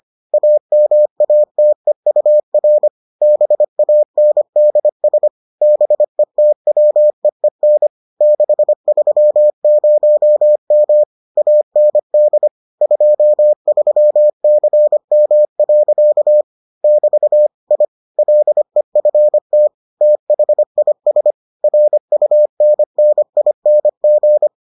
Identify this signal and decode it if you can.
Morse code, heard as a keyed tone, about 25 wpm, 600 Hz. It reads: AMATEUR BANDS BETWEEN 630M AND 23CM. <BT> I LEFT THIS RUNNING